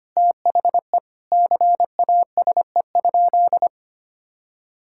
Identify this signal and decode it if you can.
Morse code, heard as a keyed tone, about 25 wpm, 700 Hz. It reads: THE CASE?
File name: practice / 25wpm / Sentences with 4-Word Phrases - 25wpm